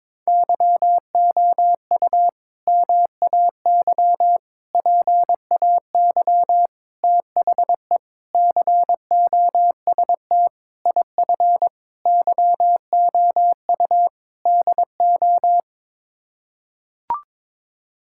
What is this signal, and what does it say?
Morse code, heard as a keyed tone, about 22 wpm, 700 Hz. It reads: YOU MAY PAY THE COST IF YOU DO
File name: practice / 22wpm / Sets of 8 Words - 22wpm